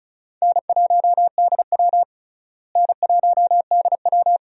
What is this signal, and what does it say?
Morse code, heard as a keyed tone, about 36 wpm, 700 Hz. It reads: N1DW N1DW
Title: USA Call Signs - 4 Characters - Repeated 2x - 35wpm